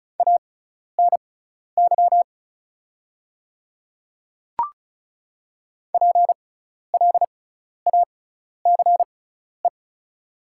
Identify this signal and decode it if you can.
Morse code, heard as a keyed tone, about 36 wpm, spaced as for 12 wpm, 690 Hz. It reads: ANY PLACE
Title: 6x Character Spacing - Top 300 Words - 35wpm